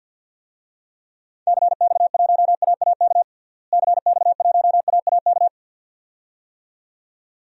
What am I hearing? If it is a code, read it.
Morse code, heard as a keyed tone, about 51 wpm, 700 Hz. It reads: CX1AAK CX1AAK
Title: DXCC-100 Call Signs - 6 Characters - Repeated 2x - 50wpm